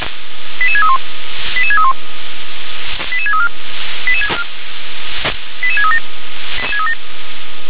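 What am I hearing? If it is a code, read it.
Five-tone selective call (ZVEI), heard as quick runs of five tones, tones of about 70 ms. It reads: 88631 88631 88634 88637
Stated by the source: ZVEI five-tone selective call received by SDR.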